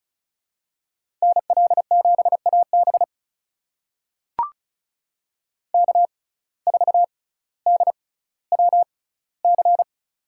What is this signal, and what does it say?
Morse code, heard as a keyed tone, about 36 wpm, spaced as for 18 wpm, 700 Hz. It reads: NL7AB K4DWC
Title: USA Call Signs - 6x Character Spacing - 35wpm